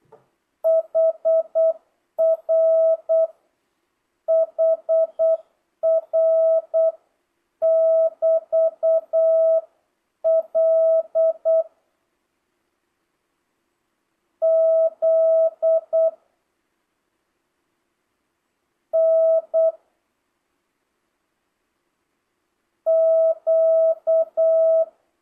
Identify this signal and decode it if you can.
Morse code, heard as a keyed tone, about 8 wpm, 650 Hz. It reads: HR HR <BT> L Z N Q
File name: jh8cbh / 2025cw